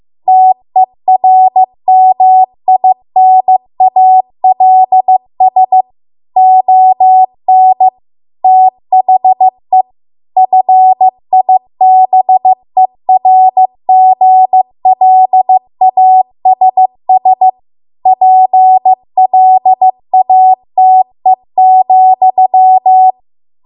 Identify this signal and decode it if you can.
Morse code, heard as a keyed tone, about 15 wpm, 750 Hz. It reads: TERMINALS ON THE FIBERGLASS PLATE,
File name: Archive / 15 WPM / 15WPM